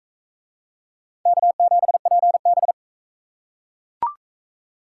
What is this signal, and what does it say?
Morse code, heard as a keyed tone, about 38 wpm, 700 Hz. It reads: K7PB